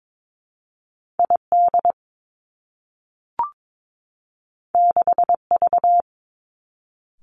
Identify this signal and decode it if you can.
Morse code, heard as a keyed tone, about 22 wpm, 700 Hz. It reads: I D 6 V
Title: Two Challenging Letter-Number Combinations 22wpm